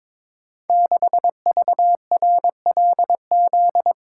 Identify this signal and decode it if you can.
Morse code, heard as a keyed tone, about 22 wpm, 700 Hz. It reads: 6VRLZ
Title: Code Groups - 5 Letter:Numbers - 22wpm